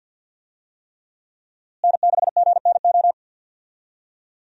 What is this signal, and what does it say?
Morse code, heard as a keyed tone, about 49 wpm, 700 Hz. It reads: N6GNO
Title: USA Call Signs - 5 Characters - Repeated 3x - 50wpm